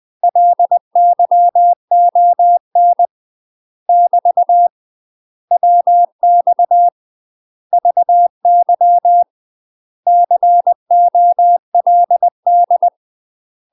Morse code, heard as a keyed tone, about 20 wpm, 700 Hz. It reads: LYON <BT> WX VY COLD
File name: UK MASTER QSOs TRAINING/20WPM